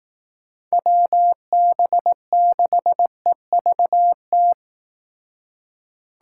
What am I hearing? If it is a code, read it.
Morse code, heard as a keyed tone, about 18 wpm, 700 Hz. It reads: WB6EVT
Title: Speed Racing - USA 2x3 Call Signs 15wpm